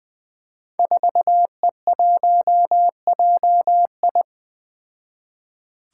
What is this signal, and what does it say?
Morse code, heard as a keyed tone, about 20 wpm, 700 Hz. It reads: 4E1JI
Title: Five Challenging Letter-Number Combinations 20wpm